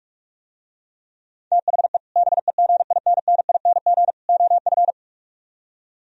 Morse code, heard as a keyed tone, about 44 wpm, 700 Hz. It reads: THE BEGINNING OF